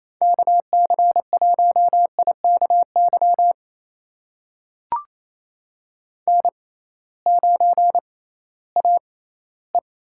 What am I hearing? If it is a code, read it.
Morse code, heard as a keyed tone, about 28 wpm, spaced as for 16 wpm, 700 Hz. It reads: KC1IKY N9AE
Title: USA Call Signs - 6x Character Spacing - 28wpm